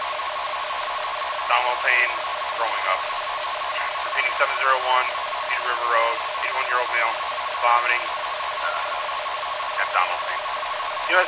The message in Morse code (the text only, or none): none